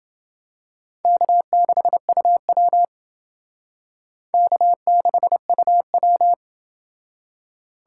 Morse code, heard as a keyed tone, about 30 wpm, 700 Hz. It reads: K6UW K6UW